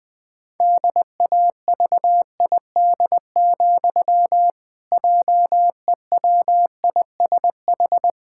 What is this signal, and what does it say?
Morse code, heard as a keyed tone, about 21 wpm, 700 Hz. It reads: DAVID, JEWISH
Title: Pangram Challenge v2 - 20wpm